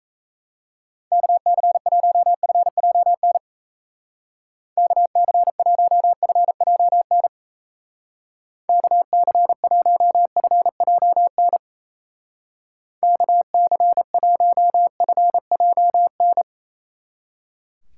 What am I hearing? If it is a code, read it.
Morse code, heard as a keyed tone, about 34 wpm, 700 Hz. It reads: KC1FJN KC1FJN KC1FJN KC1FJN
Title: Speed Racing - USA 2x3 Call Signs 28wpm